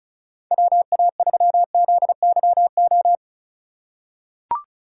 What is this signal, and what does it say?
Morse code, heard as a keyed tone, about 34 wpm, 700 Hz. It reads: WA3ZYO
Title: USA Call Signs - 6 Characters - 2x Character Spacing - 35wpm